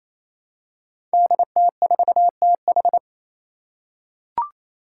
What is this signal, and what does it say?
Morse code, heard as a keyed tone, about 27 wpm, 700 Hz. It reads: DT4TH
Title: Five Challenging Letter-Number Combinations 28wpm